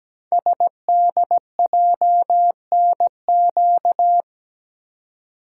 Morse code, heard as a keyed tone, about 17 wpm, 700 Hz. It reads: SDJNQ
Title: Code Groups - 5 Letter:Numbers - 17wpm